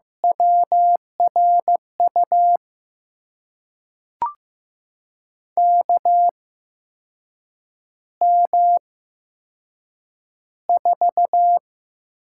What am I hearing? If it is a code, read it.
Morse code, heard as a keyed tone, about 15 wpm, spaced as for 6 wpm, 700 Hz. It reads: WRU KM4
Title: USA Call Signs - 6 Characters - 8x Character Spacing - 15wpm